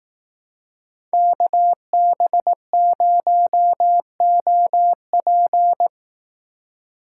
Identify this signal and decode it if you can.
Morse code, heard as a keyed tone, about 18 wpm, 700 Hz. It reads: KB0OP
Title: Speed Racing - USA Call Signs 15wpm